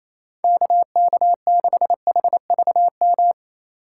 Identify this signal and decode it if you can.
Morse code, heard as a keyed tone, about 27 wpm, 700 Hz. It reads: KK6HVM
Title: USA Call Signs - 6 Characters - 3x Character Spacing - 28wpm